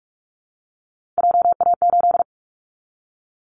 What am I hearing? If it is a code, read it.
Morse code, heard as a keyed tone, about 44 wpm, 700 Hz. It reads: JA8
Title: Callsign Prefixes for Top 25 DXCC - Repeated 3x - 45wpm